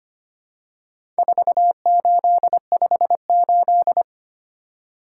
Morse code, heard as a keyed tone, about 25 wpm, 700 Hz. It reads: 4858
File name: practice / 25wpm / Four Letter-Number Combinations - 25wpm